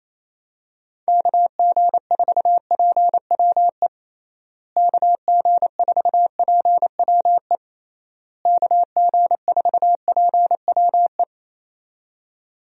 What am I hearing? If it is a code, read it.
Morse code, heard as a keyed tone, about 28 wpm, 700 Hz. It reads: KG4PWE KG4PWE KG4PWE
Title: USA Call Signs - 6 Characters - Repeated 3x - 28wpm